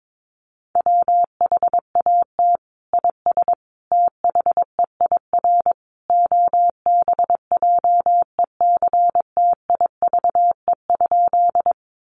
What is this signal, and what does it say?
Morse code, heard as a keyed tone, about 22 wpm, 700 Hz. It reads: WHAT IS THEIR OBJECTIVE?
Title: Sentences from Top 1200 Words 22wpm